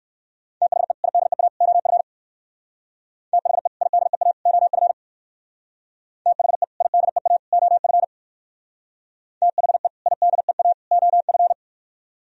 Morse code, heard as a keyed tone, about 58 wpm, 700 Hz. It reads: THE IDEA OF THE IDEA OF THE IDEA OF THE IDEA OF